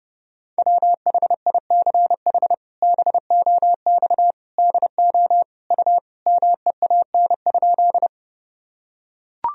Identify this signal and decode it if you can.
Morse code, heard as a keyed tone, about 30 wpm, 700 Hz. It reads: WHICH BOX DO U MEAN?
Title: Sentences from Top 500 Words and Abbreviations 30wpm